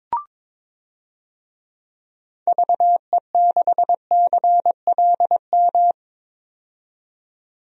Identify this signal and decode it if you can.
Morse code, heard as a keyed tone, about 21 wpm, 700 Hz. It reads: VE6CLM
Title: DXCC-100 Call Signs - 6 Characters - 22wpm